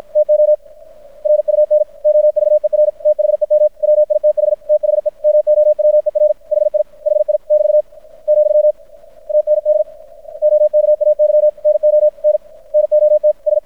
Morse code, heard as a keyed tone, about 53 wpm, 600 Hz. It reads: TO RUN OVER THEM WITH THE MOWER HI HI <BT> , TND OKAY NO N NOT S
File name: archived / CRSnet